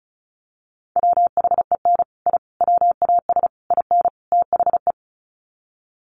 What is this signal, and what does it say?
Morse code, heard as a keyed tone, about 34 wpm, 700 Hz. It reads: WHEN I WAS IN THE
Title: Sets of 5 Words Encore2 35wpm